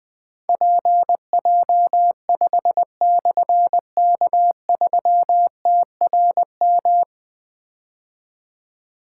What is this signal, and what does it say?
Morse code, heard as a keyed tone, about 21 wpm, 700 Hz. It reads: PJ5/K3TRM
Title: Difficult Call Signs 20wpm